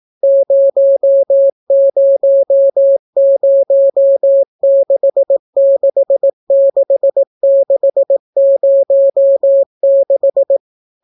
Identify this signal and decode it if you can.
Morse code, heard as a keyed tone, about 18 wpm, 550 Hz. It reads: 000666606